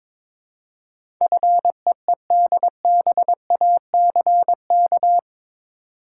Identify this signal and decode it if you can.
Morse code, heard as a keyed tone, about 22 wpm, 700 Hz. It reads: FEEDBACK
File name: practice / 22wpm / Eight Letter Words - 22wpm